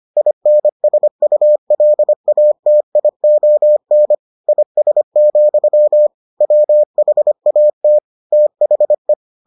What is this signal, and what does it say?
Morse code, heard as a keyed tone, about 25 wpm, 600 Hz. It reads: INSULATION IS, WHAT THE